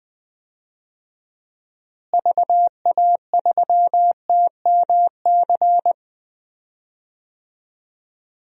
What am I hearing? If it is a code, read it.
Morse code, heard as a keyed tone, about 20 wpm, 700 Hz. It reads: VA3TMC
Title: DXCC-100 Call Signs - 6 Characters - Repeated 2x - 20wpm